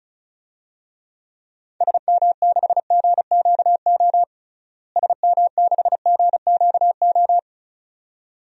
Morse code, heard as a keyed tone, about 36 wpm, 700 Hz. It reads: SM6GQO SM6GQO
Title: DXCC-100 Call Signs - 6 Characters - Repeated 2x - 35wpm